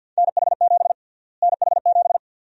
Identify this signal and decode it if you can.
Morse code, heard as a keyed tone, about 51 wpm, 700 Hz. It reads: NH7 NH7
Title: USA Call Sign Prefixes - Repeated 2x - 50wpm